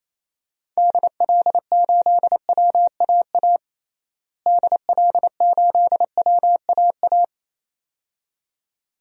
Morse code, heard as a keyed tone, about 28 wpm, 700 Hz. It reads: DL8WAA DL8WAA
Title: DXCC-100 Call Signs - 6 Characters - Repeated 2x - 28wpm